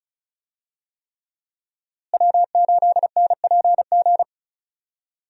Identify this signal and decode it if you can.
Morse code, heard as a keyed tone, about 34 wpm, 700 Hz. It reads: W8NPG
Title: USA Call Signs - 5 Characters - Repeated 3x - 35wpm